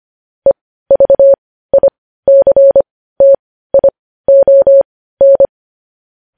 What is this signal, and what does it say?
Morse code, heard as a keyed tone, about 25 wpm, spaced as for 15 wpm, 550 Hz. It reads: EVICTION
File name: podcast / News headlines-1x-25x15